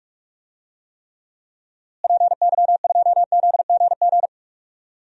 Morse code, heard as a keyed tone, about 41 wpm, 700 Hz. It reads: PY2ZGG
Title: DXCC-100 Call Signs - 6 Characters - Repeated 2x - 45wpm